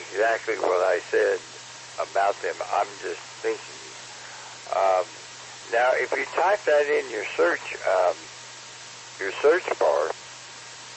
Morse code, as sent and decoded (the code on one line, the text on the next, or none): none